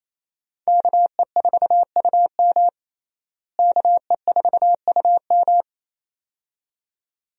Morse code, heard as a keyed tone, about 28 wpm, 700 Hz. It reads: KE4UM KE4UM
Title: USA Call Signs - 5 Characters - Repeated 2x - 28wpm